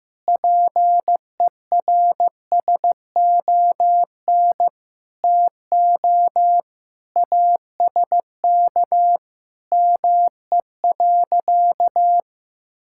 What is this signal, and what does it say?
Morse code, heard as a keyed tone, about 15 wpm, 700 Hz. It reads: PERSON TO ASK ME.
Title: Sentences from Top 1000 Words - Review Part 5 - 15wpm